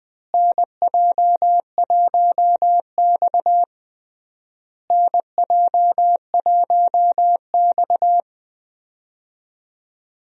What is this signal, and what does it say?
Morse code, heard as a keyed tone, about 19 wpm, 700 Hz. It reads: NJ1X NJ1X
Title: USA Call Signs - 4 Characters - Repeated 3x - 20wpm